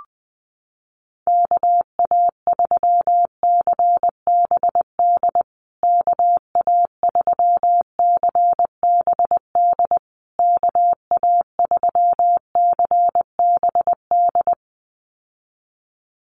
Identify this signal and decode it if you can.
Morse code, heard as a keyed tone, about 20 wpm, 700 Hz. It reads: KA3CBD KA3CBD KA3CBD
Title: USA Call Signs Repeated 3x 20wpm